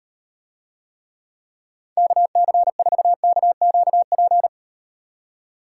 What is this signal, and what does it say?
Morse code, heard as a keyed tone, about 38 wpm, 700 Hz. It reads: KC4KQP